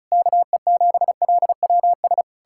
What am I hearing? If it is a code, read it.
Morse code, heard as a keyed tone, about 46 wpm, spaced as for 41 wpm, 700 Hz. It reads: KE7LWS
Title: Speed-Racing Common QSO Phrases - 35wpm